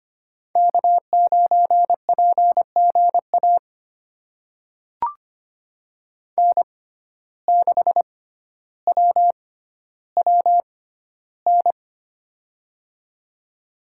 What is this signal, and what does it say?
Morse code, heard as a keyed tone, about 25 wpm, spaced as for 13 wpm, 700 Hz. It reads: K9PGA N6WWN
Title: USA Call Signs - 5 Characters - 6x Character Spacing - 25wpm